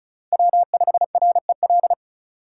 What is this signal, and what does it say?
Morse code, heard as a keyed tone, about 46 wpm, 700 Hz. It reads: W5REL